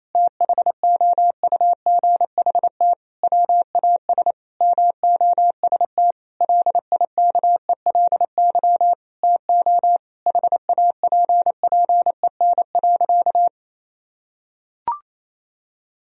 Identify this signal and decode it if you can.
Morse code, heard as a keyed tone, about 28 wpm, 700 Hz. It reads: THOUGHT WAS MOST LIKELY TO HAPPEN.
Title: Sentences from Top 1000 Words - Review Part 4 - 28wpm